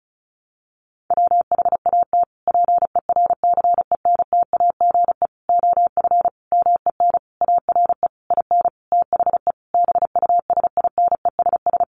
Morse code, heard as a keyed tone, about 34 wpm, 700 Hz. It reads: WHAT PERCENTAGE OF MEN ARE IN THE BUSINESS
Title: Sentences from Top 1300 Words 35wpm